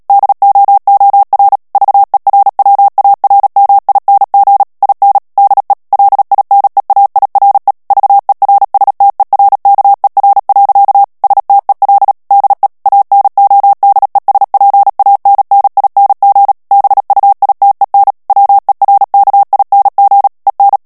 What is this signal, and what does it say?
Morse code, heard as a keyed tone, about 37 wpm, 800 Hz. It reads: DOOR VERWARMING IN DE LINEAIRE VERSTERKER. STEL DE ANODESPANNING BUITEN WERKING EN
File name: morse mp3 / HOLLANDAIS texte clair 2 vitesse 35 mots minute